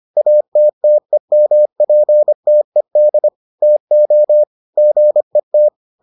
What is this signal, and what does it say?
Morse code, heard as a keyed tone, about 25 wpm, 600 Hz. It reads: ATTEMPTED TO GET